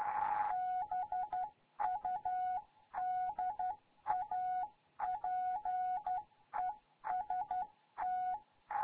Morse code, heard as a keyed tone, about 12 wpm, 740 Hz. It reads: BUDAPEST
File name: Morze-HA2MN